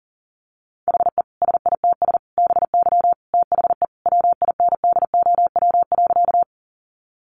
Morse code, heard as a keyed tone, about 40 wpm, 700 Hz. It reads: HE SITS BY THE WINDOW.
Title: Additional Easy Phrases 40wpm